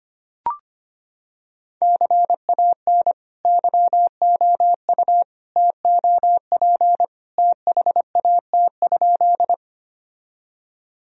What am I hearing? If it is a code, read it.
Morse code, heard as a keyed tone, about 25 wpm, 700 Hz. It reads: CAN YOU TOP THAT?
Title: Sentences from Top 1000 Words - Review Part 2 - 25wpm